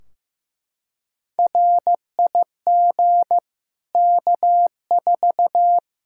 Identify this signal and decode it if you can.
Morse code, heard as a keyed tone, about 15 wpm, 700 Hz. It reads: RIG K4
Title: Practice QSO - Exchange 2 - Info - 15wpm